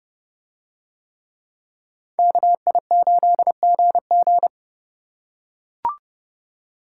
Morse code, heard as a keyed tone, about 29 wpm, 700 Hz. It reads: KI8GG